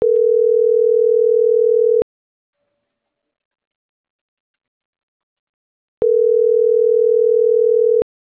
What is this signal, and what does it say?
Ringback tone (North America), 440 + 480 Hz.
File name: CENTRAL MAYAB / out atzhyry.gomez - .89805